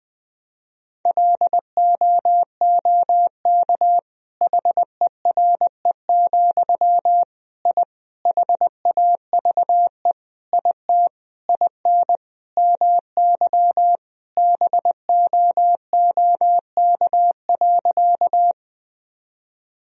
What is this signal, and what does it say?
Morse code, heard as a keyed tone, about 20 wpm, 700 Hz. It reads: LOOK HERE, I HAVE IT IN MY BOOK.
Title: Sentences from Top 1000 Words - Review Part 1 - 20wpm